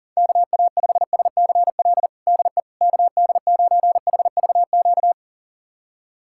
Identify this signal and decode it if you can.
Morse code, heard as a keyed tone, about 37 wpm, 690 Hz. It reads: KA5SCL DE KD9HVQ
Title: USA Callsign DE Callsign - 6 Characters - 40wpm